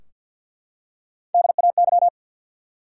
Morse code, heard as a keyed tone, about 53 wpm, 700 Hz. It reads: DAY
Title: Three Letter Words - 50wpm